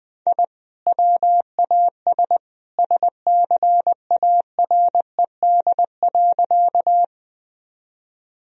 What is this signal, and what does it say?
Morse code, heard as a keyed tone, about 20 wpm, 700 Hz. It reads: I WAS SCARED.
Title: Sentences from Top 1000 Words - Review Part 4 - 20wpm